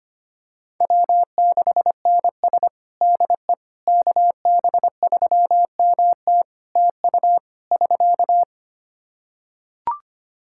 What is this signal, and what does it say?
Morse code, heard as a keyed tone, about 25 wpm, 700 Hz. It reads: W6NS DE KB3MT TU <SK>
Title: Practice QSO - Exchange 4 - Ending - 25wpm